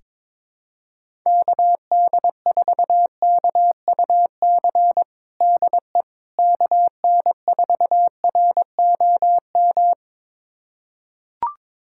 Morse code, heard as a keyed tone, about 22 wpm, 700 Hz. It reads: KD4KUC DE KN4ROM
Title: USA Callsign DE Callsign 22wpm